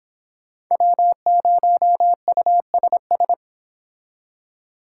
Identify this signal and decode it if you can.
Morse code, heard as a keyed tone, about 31 wpm, 700 Hz. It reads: W0USS